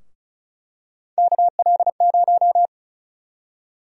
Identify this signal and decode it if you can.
Morse code, heard as a keyed tone, about 34 wpm, 700 Hz. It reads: KL0
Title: USA Call Sign Prefixes - Repeated 2x - 35wpm